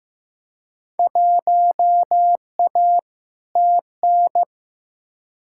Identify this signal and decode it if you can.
Morse code, heard as a keyed tone, about 15 wpm, 700 Hz. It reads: 1A TN
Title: Field Day Exchanges - 15wpm